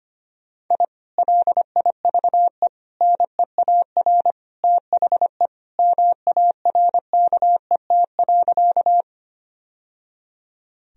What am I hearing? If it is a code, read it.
Morse code, heard as a keyed tone, about 25 wpm, 700 Hz. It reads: I LIVE NEAR THE MARKET.
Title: Sentences from Top 1000 Words - Review Part 3 - 25wpm